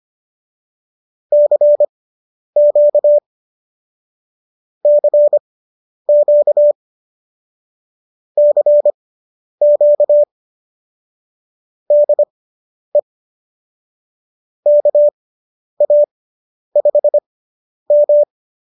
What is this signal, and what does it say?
Morse code, heard as a keyed tone, about 25 wpm, spaced as for 10 wpm, 590 Hz. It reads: CQ CQ CQ DE KA5M